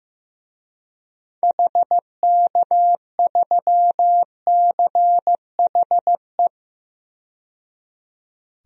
Morse code, heard as a keyed tone, about 15 wpm, 700 Hz. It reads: HK3CHE
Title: DXCC-100 Call Signs - 6 Characters - 15wpm